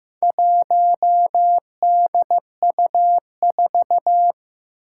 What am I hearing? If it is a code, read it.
Morse code, heard as a keyed tone, about 15 wpm, 700 Hz. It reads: 1DU4
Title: Four Challenging Letter-Number Combinations 15wpm